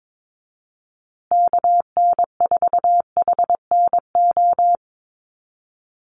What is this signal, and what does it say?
Morse code, heard as a keyed tone, about 22 wpm, 700 Hz. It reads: KN4HNO
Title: USA Call Signs Repeated 3x 22wpm